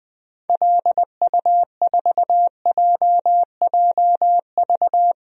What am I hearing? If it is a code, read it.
Morse code, heard as a keyed tone, about 21 wpm, 700 Hz. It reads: LU4JJV
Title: DXCC-100 Call Signs - 6 Characters - Repeated 3x - 20wpm